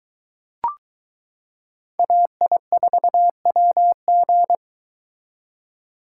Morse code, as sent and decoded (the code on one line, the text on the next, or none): .- .. ....- .-- --.
AI4WG